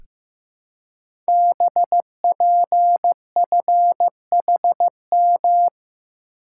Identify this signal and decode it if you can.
Morse code, heard as a keyed tone, about 15 wpm, 700 Hz. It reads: BPFHM
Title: Code Groups - 5 Letters - 15wpm